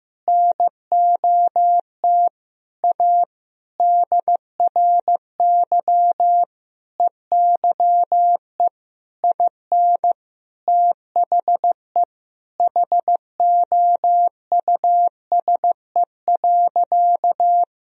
Morse code, heard as a keyed tone, about 15 wpm, 700 Hz. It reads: NOT A DRY EYE IN THE HOUSE.
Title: Sentences from Top 1000 Words - Review Part 3 - 15wpm